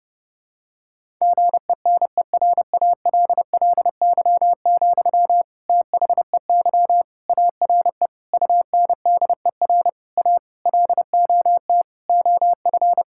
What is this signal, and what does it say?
Morse code, heard as a keyed tone, about 30 wpm, 700 Hz. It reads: GENERALLY, THEY ARE UNDER A LOT OF
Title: Sentences from Top 1000 Words - Review Part 6 - 30wpm